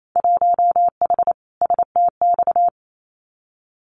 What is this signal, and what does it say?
Morse code, heard as a keyed tone, about 28 wpm, 700 Hz. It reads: 1H STX
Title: Winter Field Day Exchanges 28wpm